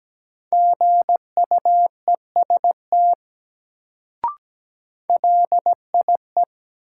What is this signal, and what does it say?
Morse code, heard as a keyed tone, about 17 wpm, 700 Hz. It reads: GUEST LIE
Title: Sentences from Top 1600 Words - New Word Drill - 17wpm